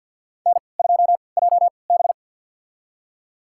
Morse code, heard as a keyed tone, about 51 wpm, spaced as for 28 wpm, 700 Hz. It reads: N2JB
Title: USA Call Signs - 4 Characters - 3x Character Spacing - 50wpm